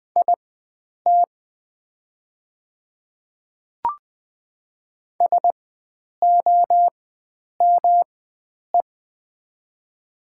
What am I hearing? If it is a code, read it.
Morse code, heard as a keyed tone, about 20 wpm, spaced as for 9 wpm, 700 Hz. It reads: IT SOME
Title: 4x Character Spacing - Top 500 Words - 20wpm